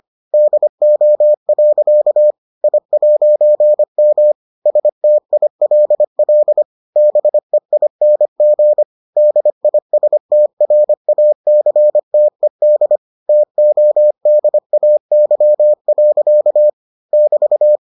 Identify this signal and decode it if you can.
Morse code, heard as a keyed tone, about 25 wpm, 600 Hz. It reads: DO. I'M STILL BEING DISTRACTED TODAY. <BT>